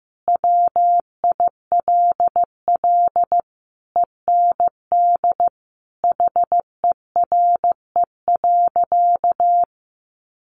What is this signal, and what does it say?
Morse code, heard as a keyed tone, about 15 wpm, 700 Hz. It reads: WILL END HERE.
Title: Sentences from Top 1200 Words 15wpm